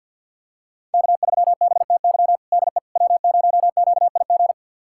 Morse code, heard as a keyed tone, about 52 wpm, 700 Hz. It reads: K3BTY DE W0QIG
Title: USA Callsign DE Callsign - 5 Characters - 50wpm